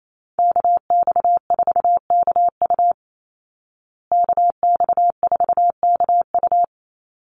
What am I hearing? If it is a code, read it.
Morse code, heard as a keyed tone, about 28 wpm, 700 Hz. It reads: KX4KU KX4KU
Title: Call Signs Repeated 2x - 28wpm